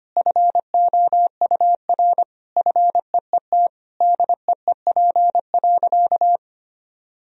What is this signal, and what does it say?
Morse code, heard as a keyed tone, about 25 wpm, 700 Hz. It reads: FOUR FEET DEEP.
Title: Sentences from Top 1000 Words - Review Part 3 - 25wpm